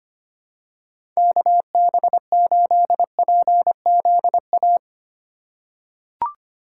Morse code, heard as a keyed tone, about 25 wpm, 690 Hz. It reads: KB8PZA
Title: USA Call Signs - 6 Characters - 2x Character Spacing - 25wpm